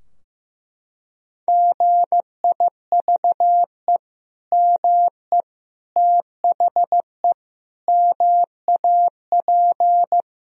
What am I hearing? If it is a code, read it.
Morse code, heard as a keyed tone, about 15 wpm, 700 Hz. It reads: GIVE ME THE MAP
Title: Sentences from Top 1000 Words - Review Part 2 - 15wpm